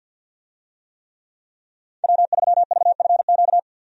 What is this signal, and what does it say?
Morse code, heard as a keyed tone, about 52 wpm, 700 Hz. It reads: W3VFQ
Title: USA Call Signs - 5 Characters - 50wpm